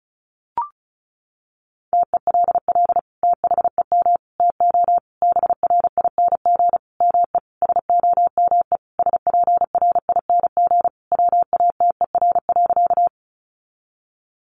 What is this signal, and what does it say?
Morse code, heard as a keyed tone, about 34 wpm, 700 Hz. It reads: TELL THEM TO BRING ME SOME SPRING WATER.
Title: Sentences from Top 1200 Words 35wpm